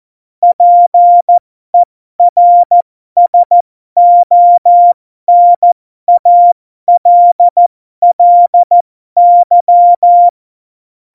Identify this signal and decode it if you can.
Morse code, heard as a keyed tone, about 14 wpm, spaced as for 12 wpm, 700 Hz. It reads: PERSONALLY
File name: Morsekurs / Witze